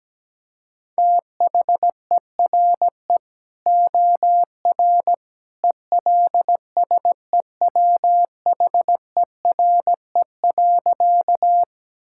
Morse code, heard as a keyed tone, about 17 wpm, 700 Hz. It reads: THERE OR ELSEWHERE.
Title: Sentences from Top 1600 Words 17wpm